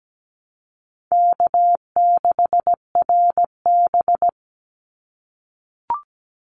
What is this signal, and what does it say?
Morse code, heard as a keyed tone, about 17 wpm, 700 Hz. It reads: K6RB
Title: Easy Call Signs - 17wpm